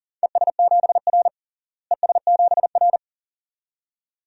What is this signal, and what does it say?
Morse code, heard as a keyed tone, about 37 wpm, 700 Hz. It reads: ES7R ES7R
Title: DXCC-100 Call Signs - 4 Characters - Repeated 2x - 40wpm